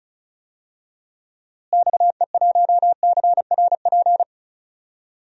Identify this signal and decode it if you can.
Morse code, heard as a keyed tone, about 34 wpm, 700 Hz. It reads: XE1CRP
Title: DXCC-100 Call Signs - 6 Characters - Repeated 3x - 35wpm